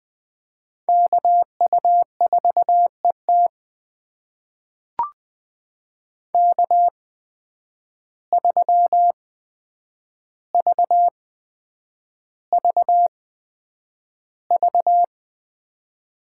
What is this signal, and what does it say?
Morse code, heard as a keyed tone, about 21 wpm, spaced as for 9 wpm, 700 Hz. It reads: KU4ET K3VVV
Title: USA Call Signs - 5 Characters - 8x Character Spacing - 20wpm